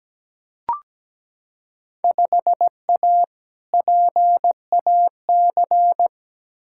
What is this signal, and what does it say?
Morse code, heard as a keyed tone, about 17 wpm, 700 Hz. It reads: 5A PAC
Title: Field Day Exchanges Encore - 17wpm